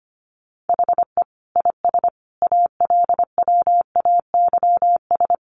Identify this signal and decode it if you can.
Morse code, heard as a keyed tone, about 25 wpm, 700 Hz. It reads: HE IS ALWAYS
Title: Sentences from Top 1300 Words 25wpm